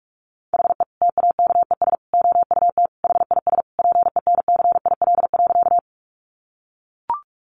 Morse code, heard as a keyed tone, about 44 wpm, 700 Hz. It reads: HE TAKES OUT HIS PENCIL.
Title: Additional Easy Phrases 45wpm